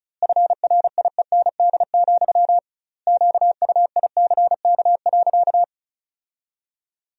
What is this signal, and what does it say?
Morse code, heard as a keyed tone, about 34 wpm, 700 Hz. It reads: FRIEND, QUICK.
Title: 2x Word-Spacing - Sentences from Top 500 Words - 35wpm